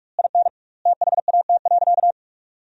Morse code, heard as a keyed tone, about 46 wpm, 700 Hz. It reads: IN THAT.
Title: 2x Word-Spacing - Sentences from Top 200 Words - 45wpm